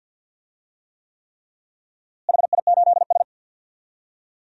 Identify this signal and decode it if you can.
Morse code, heard as a keyed tone, about 50 wpm, 700 Hz. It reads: HI8S
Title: DXCC-100 Call Signs - 4 Characters - Repeated 3x - 50wpm